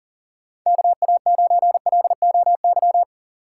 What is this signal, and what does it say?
Morse code, heard as a keyed tone, about 40 wpm, 700 Hz. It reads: KA9LOY
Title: USA Call Signs - 6 Characters - 6x Character Spacing - 40wpm